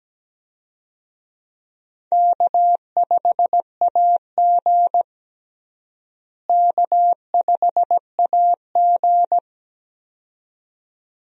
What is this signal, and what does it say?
Morse code, heard as a keyed tone, about 17 wpm, 700 Hz. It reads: K5AG K5AG
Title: USA Call Signs - 4 Characters - Repeated 2x - 17wpm